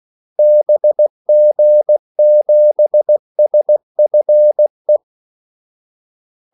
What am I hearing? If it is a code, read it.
Morse code, heard as a keyed tone, about 16 wpm, 600 Hz. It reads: BG7SFE
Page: Kaldesignaler 16 wpm